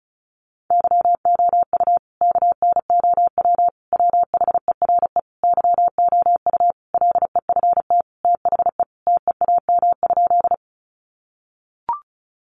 Morse code, heard as a keyed tone, about 34 wpm, 700 Hz. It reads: YOU KNOW WHERE YOU LEFT THE TEAM?
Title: Sentences from Top 1500 Words 35wpm